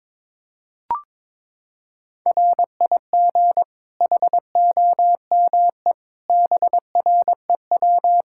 Morse code, heard as a keyed tone, about 22 wpm, 700 Hz. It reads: RIG HOME BREW
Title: K1USN SST QSO Practice 22wpm